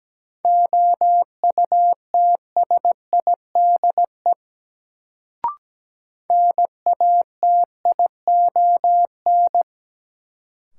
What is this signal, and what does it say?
Morse code, heard as a keyed tone, about 17 wpm, 700 Hz. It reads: OUTSIDE NATION
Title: Sentences from Top 900 Words - New Word Drill - 17wpm